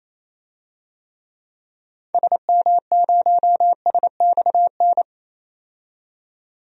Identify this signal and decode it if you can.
Morse code, heard as a keyed tone, about 28 wpm, 700 Hz. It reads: SM0SXN
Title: DXCC-100 Call Signs - 6 Characters - 28wpm